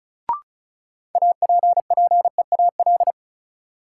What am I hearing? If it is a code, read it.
Morse code, heard as a keyed tone, about 36 wpm, 700 Hz. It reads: APPEAL
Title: Sentences from Top 1300 Words - New Word Drill - 35wpm